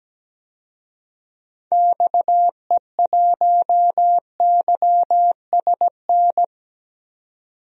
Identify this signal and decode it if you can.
Morse code, heard as a keyed tone, about 17 wpm, 700 Hz. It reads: XE1YSN
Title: DXCC-100 Call Signs - 6 Characters - 17wpm